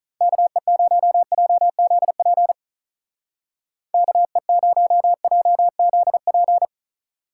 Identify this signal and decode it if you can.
Morse code, heard as a keyed tone, about 45 wpm, 700 Hz. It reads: KE0JZP KE0JZP